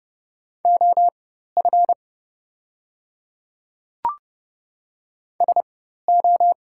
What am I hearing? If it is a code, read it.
Morse code, heard as a keyed tone, about 30 wpm, spaced as for 14 wpm, 700 Hz. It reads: OF SO
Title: 4x Character Spacing - Top 300 Words - 30wpm